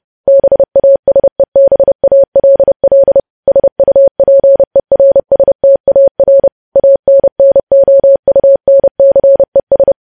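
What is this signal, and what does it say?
Morse code, heard as a keyed tone, about 30 wpm, 550 Hz. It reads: BASEBALL SUPERSTAR ANNOUNCES
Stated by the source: Morse news headlines sent at 30 wpm.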